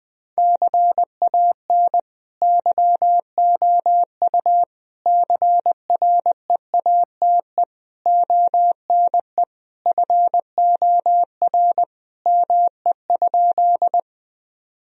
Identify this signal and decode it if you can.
Morse code, heard as a keyed tone, about 20 wpm, 700 Hz. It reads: CAN YOU CREATE ONE FOR ME?
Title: Sentences from Top 1000 Words - Review Part 3 - 20wpm